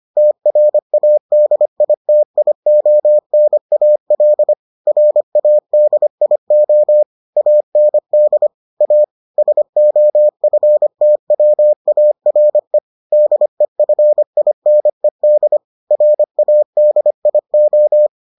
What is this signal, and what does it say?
Morse code, heard as a keyed tone, about 25 wpm, 600 Hz. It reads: TRADITIONAL RADIO AND A SOFTWARE DEFINED RADIO